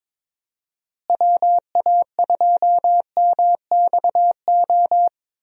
Morse code, heard as a keyed tone, about 22 wpm, 700 Hz. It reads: WA2MXO
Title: USA Call Signs - 6 Characters - Repeated 2x - 22wpm